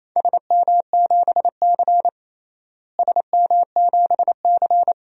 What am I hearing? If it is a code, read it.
Morse code, heard as a keyed tone, about 28 wpm, 700 Hz. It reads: SM7C SM7C
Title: DXCC-100 Call Signs - 4 Characters - Repeated 3x - 28wpm